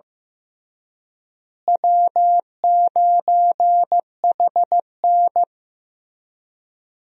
Morse code, heard as a keyed tone, about 15 wpm, 700 Hz. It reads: W9HN
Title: USA Call Signs - 4 Characters - Repeated 3x - 15wpm